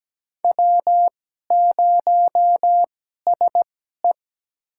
Morse code, heard as a keyed tone, about 17 wpm, spaced as for 12 wpm, 700 Hz. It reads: W0SE
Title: USA Call Signs - 2x Character Spacing - 17wpm